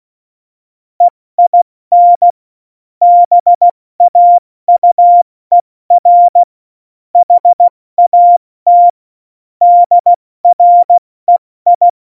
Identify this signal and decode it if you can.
Morse code, heard as a keyed tone, about 16 wpm, 700 Hz. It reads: EIN BAUER HAT DREI
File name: Morsekurs / Witze / Witz 21. Januar 2020, 70 F80